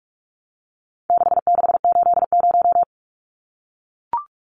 Morse code, heard as a keyed tone, about 44 wpm, 700 Hz. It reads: 6680
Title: Numbers - 4-Digit 45wpm